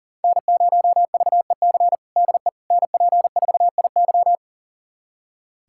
Morse code, heard as a keyed tone, about 40 wpm, 700 Hz. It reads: N0VEC DE NP4IY
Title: USA Callsign DE Callsign - 5 Characters - 40wpm